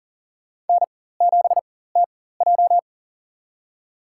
Morse code, heard as a keyed tone, about 40 wpm, spaced as for 19 wpm, 700 Hz. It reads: N7TJ
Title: USA Call Signs - 4 Characters - 4x Character Spacing - 40wpm